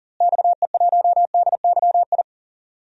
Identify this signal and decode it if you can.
Morse code, heard as a keyed tone, about 40 wpm, 700 Hz. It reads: XE1DYI
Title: DXCC-100 Call Signs - 6 Characters - Repeated 3x - 40wpm